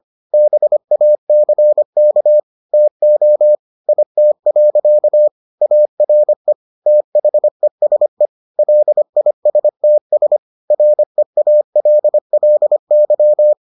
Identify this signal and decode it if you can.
Morse code, heard as a keyed tone, about 25 wpm, 600 Hz. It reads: BACK TO IT. ARE THESE LISTS REALLY